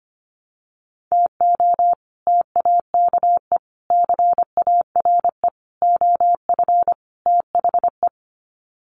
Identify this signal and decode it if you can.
Morse code, heard as a keyed tone, about 25 wpm, 700 Hz. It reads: TO TAKE CARE OF THE
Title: Sets of 5 Words Encore 25wpm